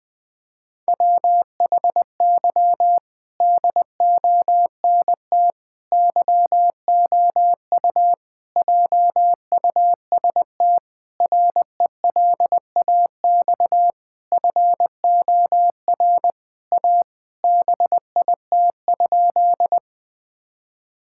Morse code, heard as a keyed tone, about 20 wpm, 700 Hz. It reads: WHY DONT YOU JUST RELAX FOR A BIT?
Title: Sentences with 5-Word Phrases - Spoken Beginning - 20wpm